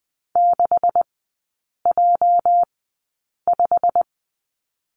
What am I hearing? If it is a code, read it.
Morse code, heard as a keyed tone, about 20 wpm, spaced as for 8 wpm, 700 Hz. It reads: 6J5
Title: 205 Lesson 33e - Letter j - Three Characters 20wpm